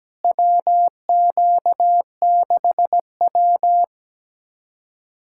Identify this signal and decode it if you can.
Morse code, heard as a keyed tone, about 17 wpm, 700 Hz. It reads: WQ6W
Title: USA Callsign DE Callsign - 4 Characters - 17wpm